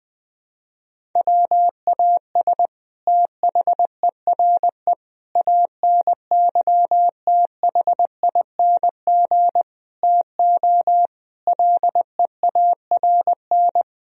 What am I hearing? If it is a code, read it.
Morse code, heard as a keyed tone, about 20 wpm, 700 Hz. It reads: WAS THERE ANYTHING TO LEARN
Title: Sentences from Top 1000 Words - Review Part 1 - 20wpm